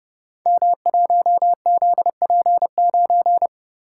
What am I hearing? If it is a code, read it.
Morse code, heard as a keyed tone, about 30 wpm, 700 Hz. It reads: M1ZP9
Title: Code Groups - 5 Letter:Numbers - 30wpm